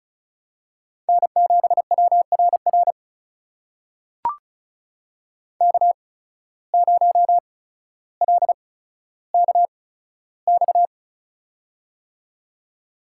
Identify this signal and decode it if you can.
Morse code, heard as a keyed tone, about 33 wpm, spaced as for 15 wpm, 700 Hz. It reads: N7WRR K0LKX
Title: USA Call Signs - 5 Characters - 8x Character Spacing - 35wpm